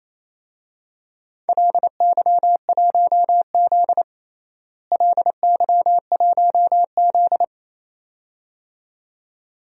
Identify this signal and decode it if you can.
Morse code, heard as a keyed tone, about 28 wpm, 700 Hz. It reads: LY1Z LY1Z
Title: DXCC-100 Call Signs - 4 Characters - Repeated 2x - 28wpm